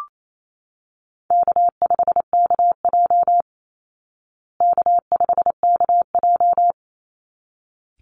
Morse code, heard as a keyed tone, about 28 wpm, 700 Hz. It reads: K5KJ K5KJ
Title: Easy Call Signs Repeated 2x - 28wpm